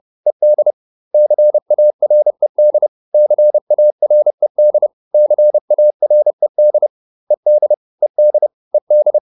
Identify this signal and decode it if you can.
Morse code, heard as a keyed tone, about 30 wpm, 600 Hz. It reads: ED CARED CARED CARED ED ED ED